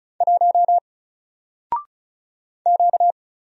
Morse code, heard as a keyed tone, about 35 wpm, 700 Hz. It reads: 1 Q
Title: Single Character (Letter:Number:Punctuation:Prosign) 35wpm